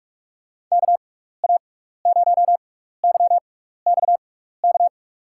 Morse code, heard as a keyed tone, about 46 wpm, spaced as for 16 wpm, 700 Hz. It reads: KA0YXK
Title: USA Call Signs - 6 Characters - 6x Character Spacing - 45wpm